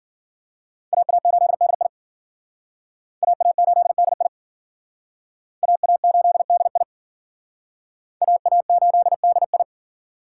Morse code, heard as a keyed tone, about 51 wpm, 700 Hz. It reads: AA8DI AA8DI AA8DI AA8DI